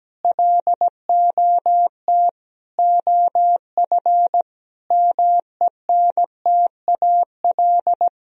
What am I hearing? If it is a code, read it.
Morse code, heard as a keyed tone, about 17 wpm, 700 Hz. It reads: LOT OF MENTAL
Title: Sentences from Top 1000 Words - Review Part 4 - 17wpm